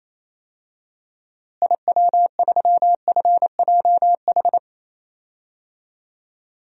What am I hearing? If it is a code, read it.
Morse code, heard as a keyed tone, about 28 wpm, 700 Hz. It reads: IW3FJH
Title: DXCC-100 Call Signs - 6 Characters - Repeated 3x - 28wpm